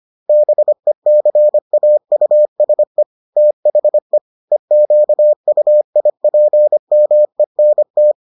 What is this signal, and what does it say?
Morse code, heard as a keyed tone, about 25 wpm, 600 Hz. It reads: BECAUSE THE EQUIPMENT